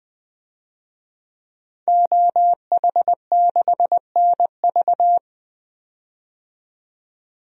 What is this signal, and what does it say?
Morse code, heard as a keyed tone, about 20 wpm, 700 Hz. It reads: OH6NV
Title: DXCC-100 Call Signs - 5 Characters - 20wpm